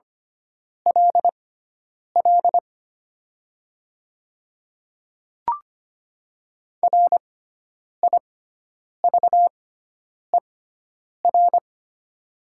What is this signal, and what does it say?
Morse code, heard as a keyed tone, about 25 wpm, spaced as for 9 wpm, 700 Hz. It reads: LL RIVER
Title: 6x Character Spacing - Top 300 Words - 25wpm